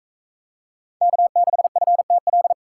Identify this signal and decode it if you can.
Morse code, heard as a keyed tone, about 41 wpm, 700 Hz. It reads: K6FTL